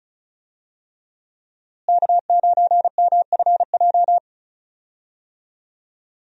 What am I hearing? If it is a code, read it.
Morse code, heard as a keyed tone, about 34 wpm, 700 Hz. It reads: K9MFJ
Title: USA Call Signs - 5 Characters - Repeated 3x - 35wpm